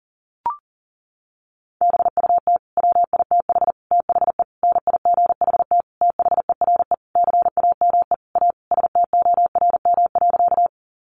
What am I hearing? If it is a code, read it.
Morse code, heard as a keyed tone, about 40 wpm, 700 Hz. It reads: BUT WITH THE NIGHT THERE CAME A STORM.
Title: Sentences from Top 1500 Words 40wpm